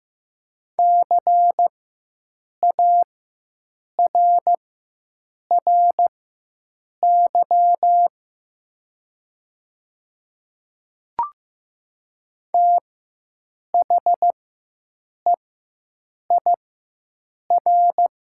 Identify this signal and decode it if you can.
Morse code, heard as a keyed tone, about 15 wpm, spaced as for 7 wpm, 700 Hz. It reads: CARRY THEIR
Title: 4x Character Spacing - Top 300 Words - 15wpm